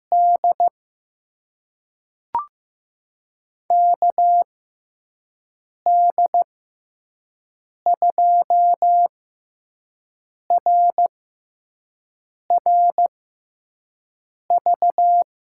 Morse code, heard as a keyed tone, about 15 wpm, spaced as for 5 wpm, 700 Hz. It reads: D KD2RRV
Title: USA Call Signs - 6 Characters - 6x Character Spacing - 15wpm